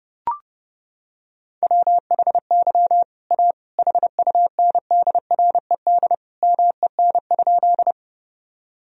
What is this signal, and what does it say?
Morse code, heard as a keyed tone, about 30 wpm, 700 Hz. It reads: WHY A HUNDRED MEN?
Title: Sentences from Top 1000 Words - Review Part 2 - 30wpm